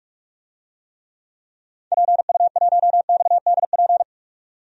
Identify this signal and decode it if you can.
Morse code, heard as a keyed tone, about 42 wpm, 700 Hz. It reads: PU1XDP
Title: DXCC-100 Call Signs - 6 Characters - 45wpm